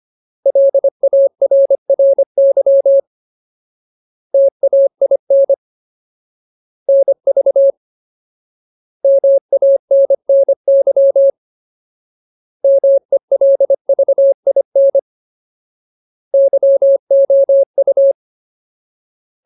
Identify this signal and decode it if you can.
Morse code, heard as a keyed tone, about 25 wpm, 550 Hz. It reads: LARRY TAIN NV MANNY MELVIN YOU
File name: Ultimate Word List / 25WPM Ultimate